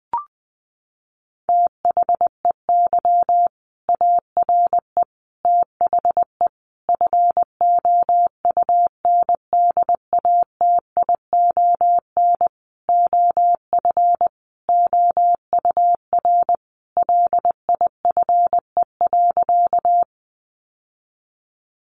Morse code, heard as a keyed tone, about 20 wpm, 700 Hz. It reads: THEY ARE THE FOUNDATION OF OUR LIFE.
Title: Sentences from Top 1200 Words 20wpm